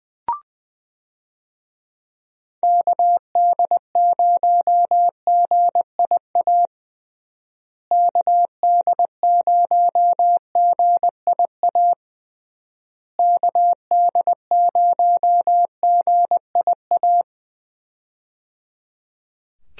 Morse code, heard as a keyed tone, about 21 wpm, 700 Hz. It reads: KD0GIA KD0GIA KD0GIA
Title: USA Call Signs - 6 Characters - Repeated 3x - 20wpm